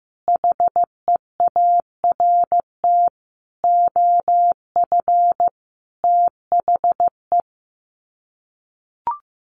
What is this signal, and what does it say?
Morse code, heard as a keyed tone, about 15 wpm, 700 Hz. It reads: HEART OF THE
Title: Sets of 5 Words 15wpm